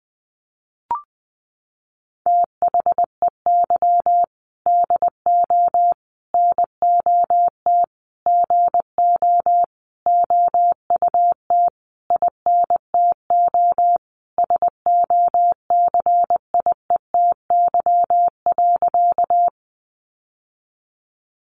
Morse code, heard as a keyed tone, about 20 wpm, 700 Hz. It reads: THEY DO NOT GO OUT INTO SOCIETY.
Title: Sentences from Top 700 Words 20wpm